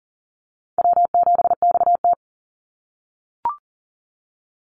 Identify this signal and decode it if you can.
Morse code, heard as a keyed tone, about 40 wpm, 700 Hz. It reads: W7XT
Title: Easy Call Signs - 40wpm